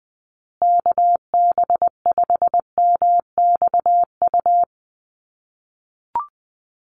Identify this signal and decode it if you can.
Morse code, heard as a keyed tone, about 20 wpm, 700 Hz. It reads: KB5MXU
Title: USA Call Signs Repeated 2x 20wpm